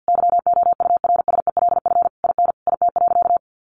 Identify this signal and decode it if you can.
Morse code, heard as a keyed tone, about 50 wpm, 700 Hz. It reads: YOURSELF IN IT.
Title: Sentences from Top 1300 Words 50wpm